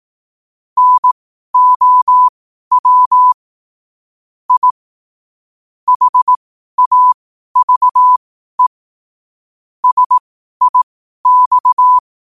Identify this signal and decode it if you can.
Morse code, heard as a keyed tone, about 18 wpm, spaced as for 12 wpm, 1000 Hz. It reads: NOW I HAVE SIX
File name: qotdfiles / 12wpm